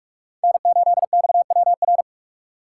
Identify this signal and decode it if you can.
Morse code, heard as a keyed tone, about 46 wpm, 700 Hz. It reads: N8XWR
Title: USA Call Signs - 4x Character Spacing - 45wpm